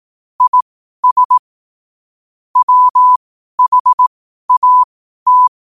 Morse code, heard as a keyed tone, about 18 wpm, spaced as for 12 wpm, 1000 Hz. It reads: IS WHAT
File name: qotdfiles / 12wpm